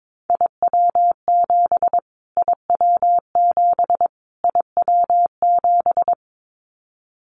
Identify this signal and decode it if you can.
Morse code, heard as a keyed tone, about 22 wpm, 700 Hz. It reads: IW7 IW7 IW7
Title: Callsign Prefixes for Top 25 DXCC - Repeated 3x - 22wpm